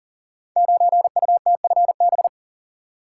Morse code, heard as a keyed tone, about 38 wpm, 690 Hz. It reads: 9UTFB
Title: Code Groups - 5 Letter:Numbers - 40wpm